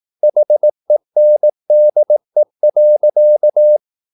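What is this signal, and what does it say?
Morse code, heard as a keyed tone, about 18 wpm, 600 Hz. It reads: HENDE.
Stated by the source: Slow (QRS) Morse practice at 18 wpm, CW med Gnister.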